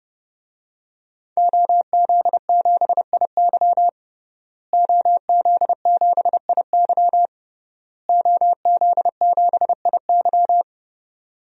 Morse code, heard as a keyed tone, about 32 wpm, 700 Hz. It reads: OZ7IY OZ7IY OZ7IY
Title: DXCC-100 Call Signs - 5 Characters - Repeated 3x - 30wpm